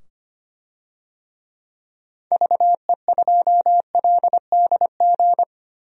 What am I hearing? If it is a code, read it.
Morse code, heard as a keyed tone, about 25 wpm, 700 Hz. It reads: VE2LDG
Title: DXCC-100 Call Signs - 6 Characters - 25wpm